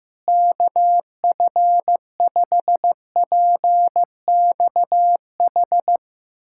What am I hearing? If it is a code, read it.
Morse code, heard as a keyed tone, about 15 wpm, 700 Hz. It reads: KF5PXH
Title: USA Callsign DE Callsign - 6 Characters - 15wpm